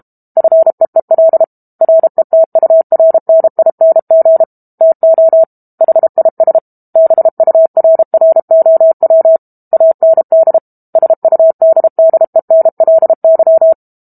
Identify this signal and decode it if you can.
Morse code, heard as a keyed tone, about 32 wpm, 650 Hz. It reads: FEEL RETURNING TO HIS BURROW AND SUDDENLY
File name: Book Two / 06-Chapter 6 - 32 WPM